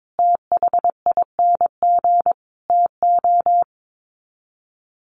Morse code, heard as a keyed tone, about 22 wpm, 700 Hz. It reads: THING TO
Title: Sets of 5 Words Encore2 22wpm